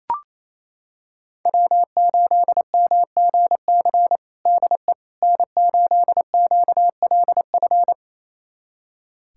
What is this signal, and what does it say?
Morse code, heard as a keyed tone, about 28 wpm, 700 Hz. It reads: W8MGC DE N8QLF
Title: USA Callsign DE Callsign - 5 Characters - 28wpm